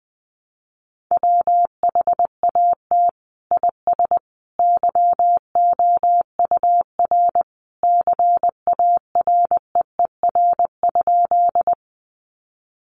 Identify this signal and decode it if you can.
Morse code, heard as a keyed tone, about 20 wpm, 700 Hz. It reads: WHAT IS YOUR CAREER?
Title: Sentences from Top 1000 Words 20wpm